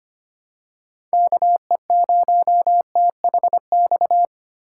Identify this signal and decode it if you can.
Morse code, heard as a keyed tone, about 25 wpm, 700 Hz. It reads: KE0THX
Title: USA Call Signs - 6 Characters - 2x Character Spacing - 25wpm